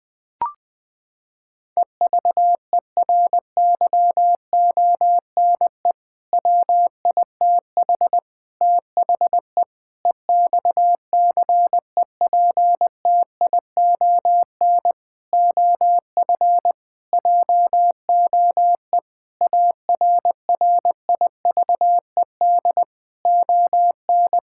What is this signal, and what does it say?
Morse code, heard as a keyed tone, about 20 wpm, 700 Hz. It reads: EVERYONE WITH THE EXCEPTION OF JOE ARRIVED ON
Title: Sentences with 4-Word Phrases - 20wpm